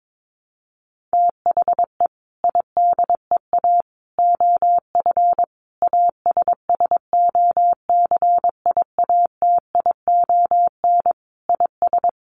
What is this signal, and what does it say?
Morse code, heard as a keyed tone, about 22 wpm, 700 Hz. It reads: THE IDEA OF ASSOCIATION IS
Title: Sentences from Top 1100 Words 22wpm